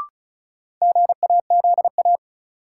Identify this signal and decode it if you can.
Morse code, heard as a keyed tone, about 36 wpm, 700 Hz. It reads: GAZA
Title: Sets of 5 Words - New Word Drill - 35wpm